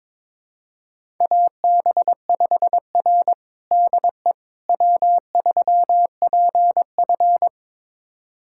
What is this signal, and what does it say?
Morse code, heard as a keyed tone, about 22 wpm, 700 Hz. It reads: AB5R DE W3PF
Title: USA Callsign DE Callsign - 4 Characters - 22wpm